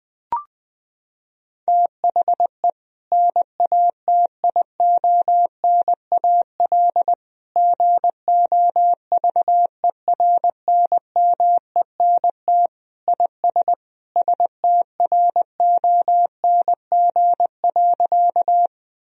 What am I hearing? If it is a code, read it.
Morse code, heard as a keyed tone, about 20 wpm, 700 Hz. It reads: THE NATIONAL GOVERNMENT IS STRONG.
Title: Sentences from Top 1000 Words - Review Part 4 - 20wpm